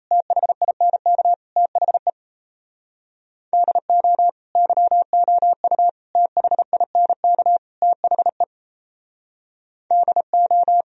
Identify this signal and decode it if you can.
Morse code, heard as a keyed tone, about 37 wpm, 700 Hz. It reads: THINK THE DO YOU THINK THE DO